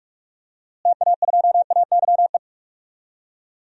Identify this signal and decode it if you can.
Morse code, heard as a keyed tone, about 46 wpm, 700 Hz. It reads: TA2AYE
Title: DXCC-100 Call Signs - 6 Characters - Repeated 2x - 45wpm